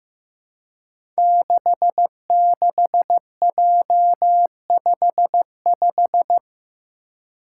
Morse code, heard as a keyed tone, about 15 wpm, 700 Hz. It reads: 66J55
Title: Five Challenging Letter-Number Combinations 15wpm